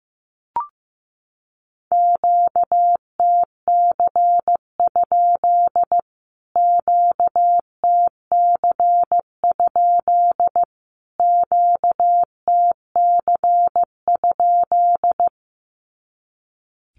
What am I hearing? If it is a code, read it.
Morse code, heard as a keyed tone, about 15 wpm, 700 Hz. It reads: QTC? QTC? QTC?
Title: Q-Codes Repeated 3x 15wpm